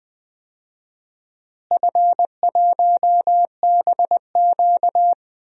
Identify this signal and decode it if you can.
Morse code, heard as a keyed tone, about 20 wpm, 700 Hz. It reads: F1BQ
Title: DXCC-100 Call Signs - 4 Characters - Repeated 2x - 20wpm